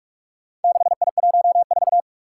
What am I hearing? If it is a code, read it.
Morse code, heard as a keyed tone, about 44 wpm, 700 Hz. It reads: 6I14
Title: Four Challenging Letter-Number Combinations 45wpm